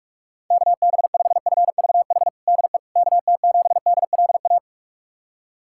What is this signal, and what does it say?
Morse code, heard as a keyed tone, about 46 wpm, 700 Hz. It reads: KB5FVH DE KT7DLA
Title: USA Callsign DE Callsign 45wpm